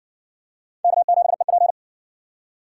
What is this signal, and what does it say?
Morse code, heard as a keyed tone, about 60 wpm, 700 Hz. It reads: K7EP